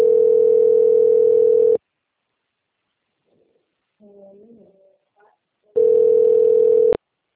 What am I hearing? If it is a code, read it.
Ringback tone (North America), 440 + 480 Hz, with speech.
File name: out- - - - - .44760